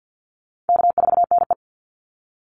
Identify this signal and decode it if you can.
Morse code, heard as a keyed tone, about 48 wpm, 700 Hz. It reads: K4NE
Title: Easy Call Signs Repeated 2x - 50wpm